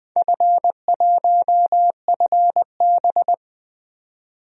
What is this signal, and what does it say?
Morse code, heard as a keyed tone, about 20 wpm, 700 Hz. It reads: F1FB
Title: DXCC-100 Call Signs - 4 Characters - 20wpm